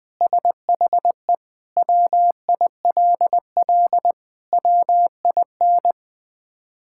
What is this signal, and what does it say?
Morse code, heard as a keyed tone, about 20 wpm, 700 Hz. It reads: SHE WILL WIN
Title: Sentences with 3-Word Phrases - 20wpm